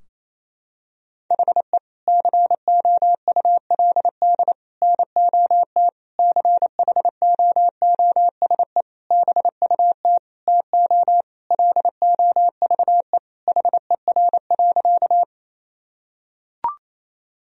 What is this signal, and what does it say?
Morse code, heard as a keyed tone, about 28 wpm, 700 Hz. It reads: HE COULD NOT CHOOSE BUT TO LOVE HER.
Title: Sentences from Top 1000 Words - Review Part 4 - 28wpm